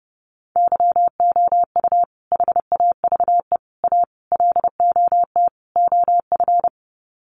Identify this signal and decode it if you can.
Morse code, heard as a keyed tone, about 30 wpm, 700 Hz. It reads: YOU HAVE A LOT OF
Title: Sets of 5 Words Encore 30wpm